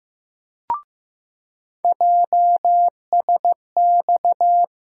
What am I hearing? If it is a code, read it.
Morse code, heard as a keyed tone, about 15 wpm, 700 Hz. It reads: JSX
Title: USA Call Sign Postfixes - 3 Letters 15wpm